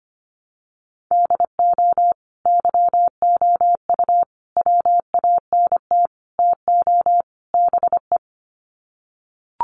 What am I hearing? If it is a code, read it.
Morse code, heard as a keyed tone, about 25 wpm, 700 Hz. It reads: DO YOU WANT TO BE
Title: Sets of 5 Words Encore2 25wpm